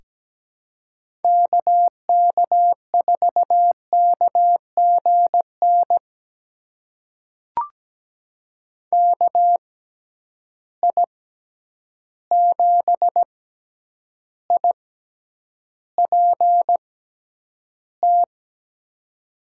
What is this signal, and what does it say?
Morse code, heard as a keyed tone, about 17 wpm, spaced as for 9 wpm, 700 Hz. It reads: KK4KGN KI7IPT
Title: USA Call Signs - 6 Characters - 6x Character Spacing - 17wpm